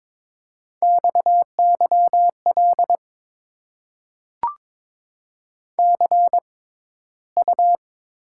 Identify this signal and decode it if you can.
Morse code, heard as a keyed tone, about 22 wpm, spaced as for 14 wpm, 700 Hz. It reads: XYL CU
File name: Top 90 QSO Elements - 6x Character Spacing - 22wpm